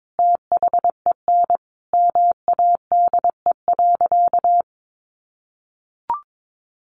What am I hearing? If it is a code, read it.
Morse code, heard as a keyed tone, about 22 wpm, 700 Hz. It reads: THEN MADE.
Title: Sentences from Top 1500 Words 22wpm